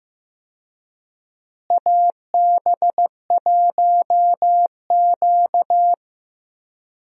Morse code, heard as a keyed tone, about 15 wpm, 700 Hz. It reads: AB1Q
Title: USA Call Signs - 4 Characters - Repeated 3x - 15wpm